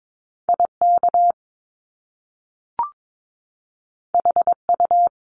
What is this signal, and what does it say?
Morse code, heard as a keyed tone, about 22 wpm, 700 Hz. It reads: IK HU
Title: Top Two Letter Combinations in Words 22wpm